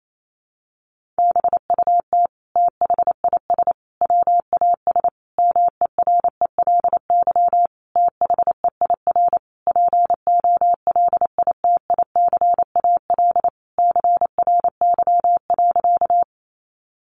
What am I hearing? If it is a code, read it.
Morse code, heard as a keyed tone, about 28 wpm, 700 Hz. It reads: BUT THIS WAS MERELY THEIR POLITICAL CRY.
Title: Sentences from Top 1400 Words 28wpm